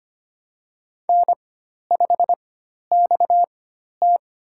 Morse code, heard as a keyed tone, about 25 wpm, spaced as for 12 wpm, 700 Hz. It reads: N5XT
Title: USA Call Signs - 4 Characters - 4x Character Spacing - 25wpm